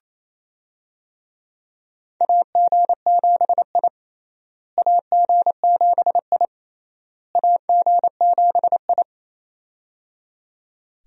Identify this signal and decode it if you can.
Morse code, heard as a keyed tone, about 28 wpm, 700 Hz. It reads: AG7I AG7I AG7I
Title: USA Call Signs - 4 Characters - Repeated 3x - 28wpm